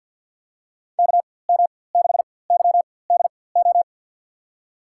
Morse code, heard as a keyed tone, about 47 wpm, spaced as for 23 wpm, 700 Hz. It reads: KM6YDO
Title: USA Call Signs - 4x Character Spacing - 50wpm